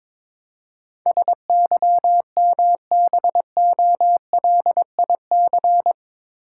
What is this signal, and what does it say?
Morse code, heard as a keyed tone, about 22 wpm, 700 Hz. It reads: SYMBOLIC
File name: practice / 22wpm / Eight Letter Words - 22wpm